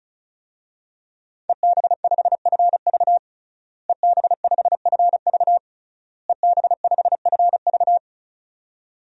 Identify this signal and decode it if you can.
Morse code, heard as a keyed tone, about 34 wpm, 700 Hz. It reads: EB5FV EB5FV EB5FV
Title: DXCC-100 Call Signs - 5 Characters - Repeated 3x - 35wpm